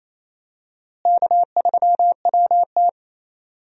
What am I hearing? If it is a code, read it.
Morse code, heard as a keyed tone, about 27 wpm, 700 Hz. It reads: K3WT
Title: USA Call Signs - 4 Characters - Repeated 3x - 28wpm